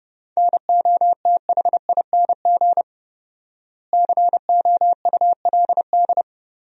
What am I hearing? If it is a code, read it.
Morse code, heard as a keyed tone, about 30 wpm, 700 Hz. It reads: NOTHING COULD
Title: 4x Word-Spacing - Sentences from Top 500 Words - 30wpm